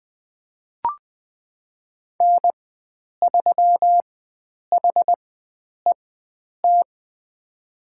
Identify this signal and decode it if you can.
Morse code, heard as a keyed tone, about 20 wpm, spaced as for 9 wpm, 700 Hz. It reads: N3HET
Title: USA Call Signs - 5 Characters - 4x Character Spacing - 20wpm